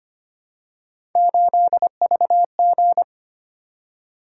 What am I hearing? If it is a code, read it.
Morse code, heard as a keyed tone, about 25 wpm, 700 Hz. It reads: 8VG
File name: Three Letter-Number Combinations - 25wpm